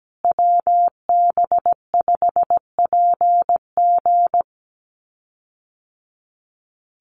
Morse code, heard as a keyed tone, about 17 wpm, 700 Hz. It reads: WB5PG
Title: USA Call Signs Repeated 2x 17wpm